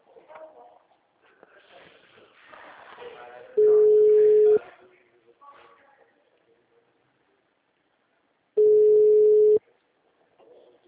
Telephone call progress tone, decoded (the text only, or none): ringback tone